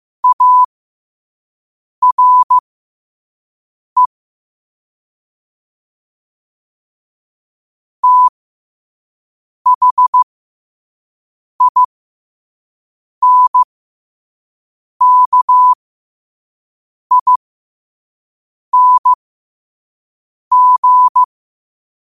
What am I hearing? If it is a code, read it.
Morse code, heard as a keyed tone, about 15 wpm, spaced as for 5 wpm, 1000 Hz. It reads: ARE THINKING
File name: qotdfiles / 5wpm